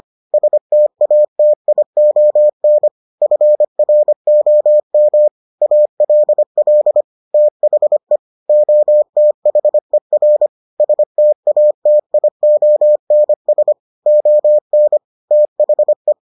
Morse code, heard as a keyed tone, about 25 wpm, 600 Hz. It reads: STATION FROM ALL THE OTHER STATIONS ON THE